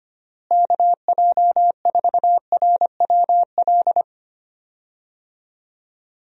Morse code, heard as a keyed tone, about 25 wpm, 700 Hz. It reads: KJ4RWL
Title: USA Call Signs - 6 Characters - Repeated 3x - 25wpm